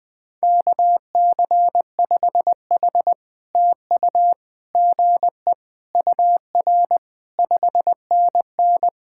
Morse code, heard as a keyed tone, about 17 wpm, 700 Hz. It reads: KC5H TU GE UR 5NN
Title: POTA QSO Practice - 20wpm